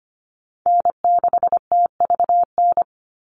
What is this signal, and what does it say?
Morse code, heard as a keyed tone, about 25 wpm, 700 Hz. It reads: N6TVN
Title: Call Signs - 25wpm